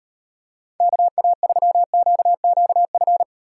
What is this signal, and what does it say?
Morse code, heard as a keyed tone, about 32 wpm, 700 Hz. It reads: KA3QQF